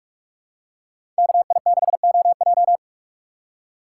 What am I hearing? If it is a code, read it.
Morse code, heard as a keyed tone, about 43 wpm, 700 Hz. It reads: KI6OJ
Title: USA Call Signs - 5 Characters - 6x Character Spacing - 45wpm